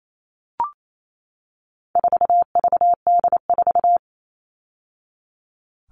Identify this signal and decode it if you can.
Morse code, heard as a keyed tone, about 28 wpm, 700 Hz. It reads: 4VD4
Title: All Cut Numbers - 4-Digit 28wpm